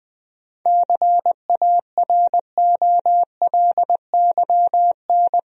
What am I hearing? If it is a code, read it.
Morse code, heard as a keyed tone, about 20 wpm, 700 Hz. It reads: CAROLYN
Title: USA Common Names - 20wpm